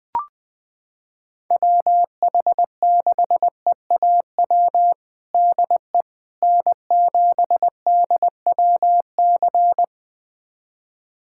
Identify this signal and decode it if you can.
Morse code, heard as a keyed tone, about 20 wpm, 700 Hz. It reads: WH6EAW DE N7DWC
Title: USA Callsign DE Callsign 20wpm